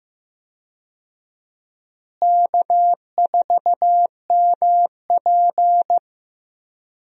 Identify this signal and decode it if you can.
Morse code, heard as a keyed tone, about 15 wpm, 700 Hz. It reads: K4MP
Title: USA Call Signs - 4 Characters - Repeated 3x - 15wpm